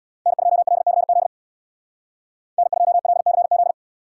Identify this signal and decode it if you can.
Morse code, heard as a keyed tone, about 69 wpm, 700 Hz. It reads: N2LCZ N2LCZ